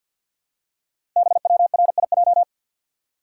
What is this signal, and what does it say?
Morse code, heard as a keyed tone, about 48 wpm, 700 Hz. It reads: BWRIJ
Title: Code Groups - 5 Letters - 50wpm